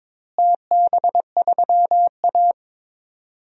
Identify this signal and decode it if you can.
Morse code, heard as a keyed tone, about 22 wpm, 700 Hz. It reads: TB3A
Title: DXCC-100 Call Signs - 4 Characters - Repeated 2x - 22wpm